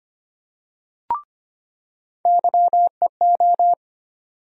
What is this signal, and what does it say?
Morse code, heard as a keyed tone, about 25 wpm, 700 Hz. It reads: YEO
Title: USA Call Sign Postfixes - 3 Letters 25wpm